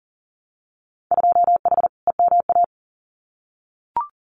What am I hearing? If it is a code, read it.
Morse code, heard as a keyed tone, about 40 wpm, 700 Hz. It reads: 2H EMA
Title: Winter Field Day Exchanges 40wpm